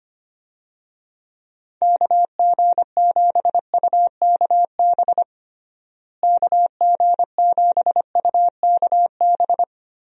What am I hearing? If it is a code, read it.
Morse code, heard as a keyed tone, about 25 wpm, 700 Hz. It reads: KG7UKB KG7UKB
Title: USA Call Signs - 6 Characters - Repeated 3x - 25wpm